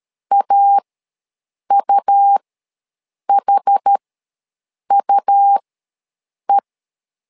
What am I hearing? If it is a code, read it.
Morse code, heard as a keyed tone, about 13 wpm, spaced as for 7 wpm, 800 Hz. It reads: AUHUE
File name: LEZIONI / 2 UHDN / 2 UH ws 12wpm 1730